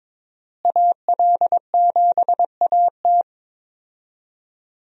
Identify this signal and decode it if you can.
Morse code, heard as a keyed tone, about 22 wpm, 700 Hz. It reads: AL7AT
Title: USA Call Signs - 5 Characters - Repeated 3x - 22wpm